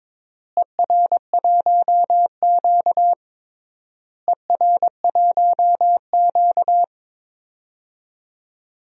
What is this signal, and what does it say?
Morse code, heard as a keyed tone, about 22 wpm, 700 Hz. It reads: ER1Q ER1Q
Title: DXCC-100 Call Signs - 4 Characters - Repeated 3x - 22wpm